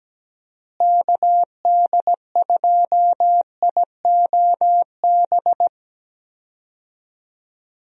Morse code, heard as a keyed tone, about 17 wpm, 700 Hz. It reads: KD2IOB
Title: USA Call Signs - 6 Characters - Repeated 3x - 17wpm